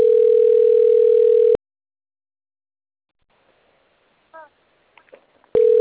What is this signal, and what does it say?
Ringback tone (North America), 440 + 480 Hz.